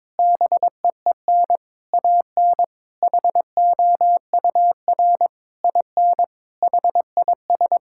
Morse code, heard as a keyed tone, about 21 wpm, 700 Hz. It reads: BEEN AN HOUR IN HIS
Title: Sentences from Top 1000 Words - Review Part 6 - 22wpm